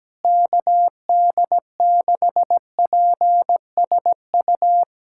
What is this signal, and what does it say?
Morse code, heard as a keyed tone, about 17 wpm, 700 Hz. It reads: KD6PSU
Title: USA Callsign DE Callsign 17wpm